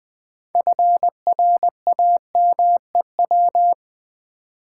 Morse code, heard as a keyed tone, about 20 wpm, 700 Hz. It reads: FRAMEW
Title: Build A Word - 3 Letters At A Time - 9-Letter Words - 20wpm